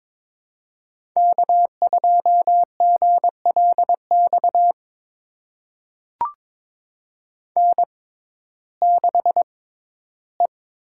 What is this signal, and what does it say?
Morse code, heard as a keyed tone, about 22 wpm, spaced as for 14 wpm, 700 Hz. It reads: K2GLX N6E
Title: USA Call Signs - 5 Characters - 6x Character Spacing - 22wpm